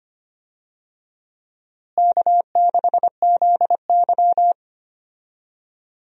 Morse code, heard as a keyed tone, about 25 wpm, 700 Hz. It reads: K6ZY
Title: USA Call Signs - 4 Characters - Repeated 2x - 25wpm